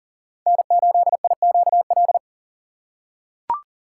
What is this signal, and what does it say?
Morse code, heard as a keyed tone, about 40 wpm, 700 Hz. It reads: N8IQL
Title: USA Call Signs - 5 Characters - 2x Character Spacing - 40wpm